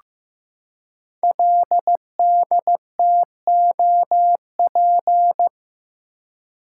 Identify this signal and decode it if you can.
Morse code, heard as a keyed tone, about 15 wpm, 700 Hz. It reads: LDTOP
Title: Code Groups - 5 Letters - 15wpm